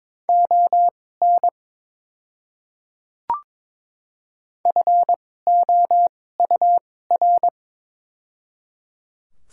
Morse code, heard as a keyed tone, about 22 wpm, spaced as for 16 wpm, 700 Hz. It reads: ON FOUR
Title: 2x Character Spacing - Top 300 Words - 22wpm